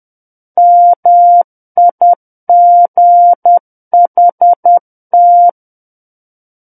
Morse code, heard as a keyed tone, about 10 wpm, 700 Hz. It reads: MIGHT
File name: Book One / 01-Chapter 1 - 10 WPM